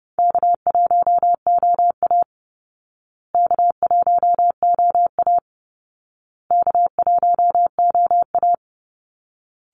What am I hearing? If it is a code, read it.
Morse code, heard as a keyed tone, about 30 wpm, 700 Hz. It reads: K1OA K1OA K1OA
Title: Call Signs Repeated 2x - 30wpm